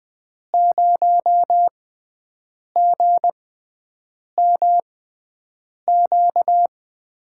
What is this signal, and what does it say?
Morse code, heard as a keyed tone, about 20 wpm, spaced as for 7 wpm, 700 Hz. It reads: 0GMQ
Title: USA Call Signs - 6 Characters - 6x Character Spacing - 20wpm